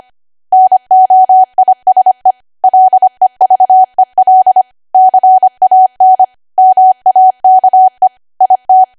Morse code, heard as a keyed tone, about 25 wpm, 750 Hz. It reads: NOISE LEVEL CAN MAKE IT
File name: Archive / 25 WPM / 25WPM